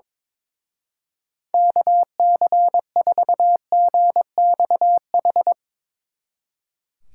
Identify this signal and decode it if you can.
Morse code, heard as a keyed tone, about 22 wpm, 700 Hz. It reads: KC4GXH
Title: Speed Racing - USA 2x3 Call Signs 22wpm